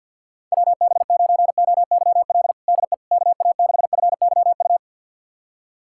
Morse code, heard as a keyed tone, about 53 wpm, 700 Hz. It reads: WB9OYL DE KA6FYU
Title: USA Callsign DE Callsign - 6 Characters - 50wpm